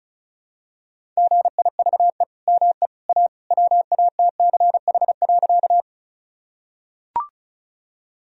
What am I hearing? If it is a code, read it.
Morse code, heard as a keyed tone, about 36 wpm, 700 Hz. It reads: GIVE ME A WATCH.
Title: Sentences from Top 1000 Words - Review Part 1 - 35wpm